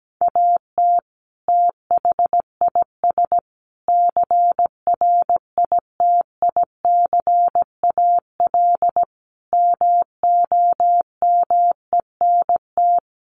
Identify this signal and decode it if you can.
Morse code, heard as a keyed tone, about 17 wpm, 700 Hz. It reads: AT THIS CRITICAL MOMENT
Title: Sentences from Top 1100 Words 17wpm